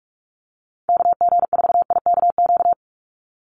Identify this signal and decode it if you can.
Morse code, heard as a keyed tone, about 44 wpm, 700 Hz. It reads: KG4IKQ
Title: Call Signs Repeated 2x - 45wpm